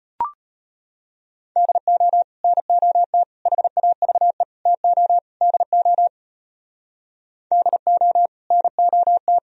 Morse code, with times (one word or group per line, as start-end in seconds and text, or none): 1.6-2.2 DO
2.4-3.2 NOT
3.5-4.4 HAVE
4.7-5.2 TO
5.4-6.1 DO
7.5-8.3 DO
8.5-9.4 NOT